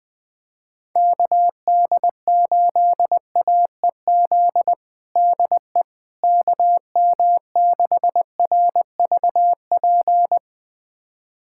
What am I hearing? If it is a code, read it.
Morse code, heard as a keyed tone, about 21 wpm, 700 Hz. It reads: KD8AEZ DE KM6RVP
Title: USA Callsign DE Callsign - 6 Characters - 20wpm